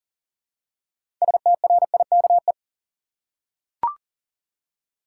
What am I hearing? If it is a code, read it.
Morse code, heard as a keyed tone, about 40 wpm, 700 Hz. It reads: STRIKE
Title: Six Letter Words - 40wpm